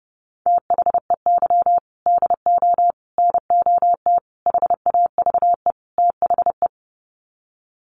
Morse code, heard as a keyed tone, about 30 wpm, 700 Hz. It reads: THEY DO NOT HAVE THE
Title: Sets of 5 Words 30wpm